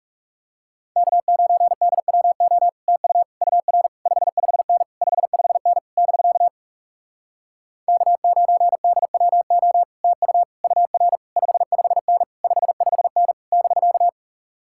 Morse code, heard as a keyed tone, about 37 wpm, 700 Hz. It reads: K9DWO TU UR 55N 55N <BK> K9DWO TU UR 55N 55N <BK>
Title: Speed-Racing Common QSO Phrases Encore - 30wpm